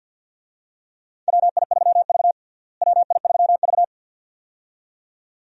Morse code, heard as a keyed tone, about 51 wpm, 700 Hz. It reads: WI3V WI3V
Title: USA Call Signs - 4 Characters - Repeated 2x - 50wpm